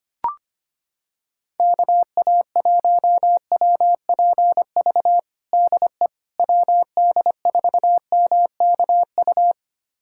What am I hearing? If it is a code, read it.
Morse code, heard as a keyed tone, about 25 wpm, 700 Hz. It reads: KA1WPV DE WD4MKU
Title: USA Callsign DE Callsign - 6 Characters - 25wpm